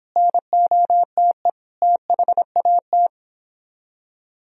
Morse code, heard as a keyed tone, about 26 wpm, 700 Hz. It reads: NOTE THAT
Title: Speed-Racing - Sets of 5 Words 22wpm